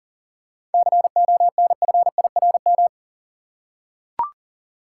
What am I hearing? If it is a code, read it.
Morse code, heard as a keyed tone, about 40 wpm, 700 Hz. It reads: CONFIRM
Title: Seven Letter Words - 40wpm